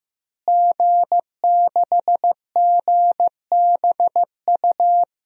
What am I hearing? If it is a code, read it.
Morse code, heard as a keyed tone, about 15 wpm, 700 Hz. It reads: G6GBU
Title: DXCC-100 Call Signs - 5 Characters - Repeated 3x - 15wpm